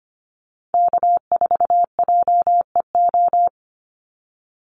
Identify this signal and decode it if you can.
Morse code, heard as a keyed tone, about 25 wpm, 690 Hz. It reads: K4JEO
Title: USA Call Signs 25wpm